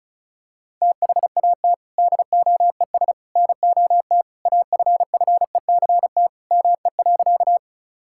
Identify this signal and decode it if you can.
Morse code, heard as a keyed tone, about 36 wpm, 700 Hz. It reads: THAT DOES NOT AFFECT ME.
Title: Sentences from Top 1000 Words - Review Part 6 - 35wpm